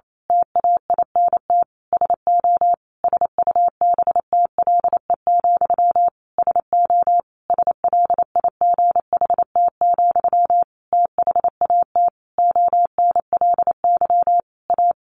Morse code, heard as a keyed tone, about 28 wpm, 700 Hz. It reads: TAINT SO SUBTLE, SO SLIGHT, THAT ONLY A
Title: A Sound Of Thunder 28wpm